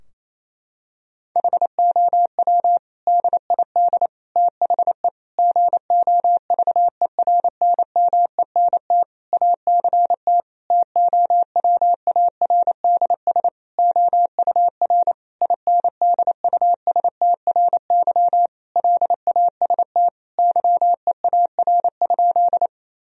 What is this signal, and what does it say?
Morse code, heard as a keyed tone, about 27 wpm, 700 Hz. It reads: HOW DID THE GOVERNMENT ACT TOWARDS OUR INDUSTRY LAST YEAR?
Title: Sentences from Top 1000 Words - Review Part 4 - 28wpm